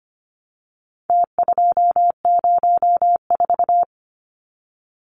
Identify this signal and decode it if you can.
Morse code, heard as a keyed tone, about 25 wpm, 700 Hz. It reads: T204
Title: Numbers - Contest Style - 4-Digit 25wpm